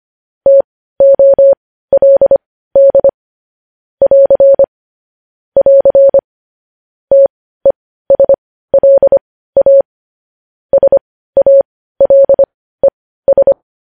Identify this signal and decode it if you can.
Morse code, heard as a keyed tone, about 25 wpm, spaced as for 15 wpm, 550 Hz. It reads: TOLD <AR> <AR> TESLA SALES
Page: news headlines in morse code at 15 wpm